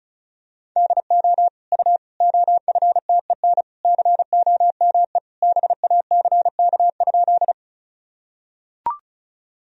Morse code, heard as a keyed tone, about 36 wpm, 700 Hz. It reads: DO U OFTEN COME BACK?
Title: Sentences from Top 300 Words and Abbreviations 35wpm